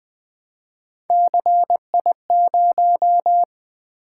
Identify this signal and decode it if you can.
Morse code, heard as a keyed tone, about 20 wpm, 700 Hz. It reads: CI0
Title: Three Letter-Number Combinations - 20wpm